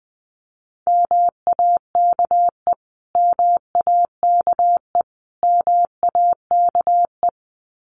Morse code, heard as a keyed tone, about 20 wpm, 700 Hz. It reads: MAKE MAKE MAKE
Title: Top 500 Words Repeated 20wpm